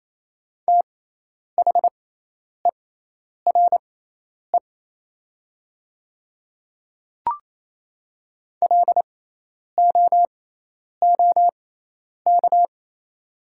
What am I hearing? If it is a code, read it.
Morse code, heard as a keyed tone, about 28 wpm, spaced as for 10 wpm, 700 Hz. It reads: THERE LOOK
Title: 6x Character Spacing - Top 300 Words - 28wpm